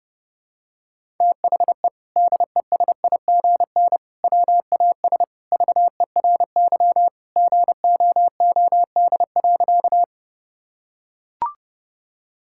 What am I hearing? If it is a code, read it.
Morse code, heard as a keyed tone, about 30 wpm, 700 Hz. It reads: THE DESIGN WAS VERY GOOD.
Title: Sentences from Top 1000 Words - Review Part 5 - 30wpm